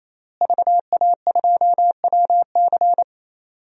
Morse code, heard as a keyed tone, about 28 wpm, 700 Hz. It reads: VA2WC
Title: DXCC-100 Call Signs - 5 Characters - Repeated 3x - 28wpm